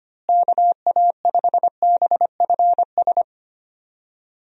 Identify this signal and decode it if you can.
Morse code, heard as a keyed tone, about 25 wpm, 700 Hz. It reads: KA5BFS
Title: USA Call Signs - 6 Characters - Repeated 2x - 25wpm